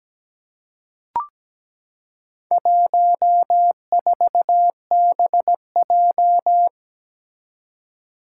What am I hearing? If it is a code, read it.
Morse code, heard as a keyed tone, about 17 wpm, 700 Hz. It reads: 14BJ
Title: Four Challenging Letter-Number Combinations 17wpm